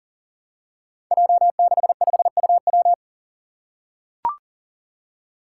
Morse code, heard as a keyed tone, about 39 wpm, 700 Hz. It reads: J65UW
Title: Five Challenging Letter-Number Combinations 40wpm